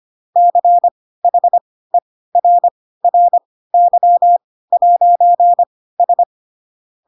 Morse code, heard as a keyed tone, about 25 wpm, spaced as for 16 wpm, 700 Hz. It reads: CHERRY'S
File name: Pinocchio / Pinocchio ch1 10 25 ik1hge